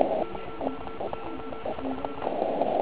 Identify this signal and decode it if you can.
Morse code, heard as a keyed tone, about 38 wpm, 1340 Hz, with noise. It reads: OK1IF
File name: Archiv / CUC129 / CUC129 pileup